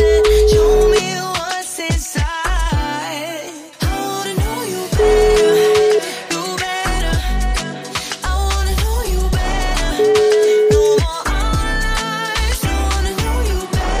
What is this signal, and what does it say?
Ringback tone (Europe, 425 Hz), with noise.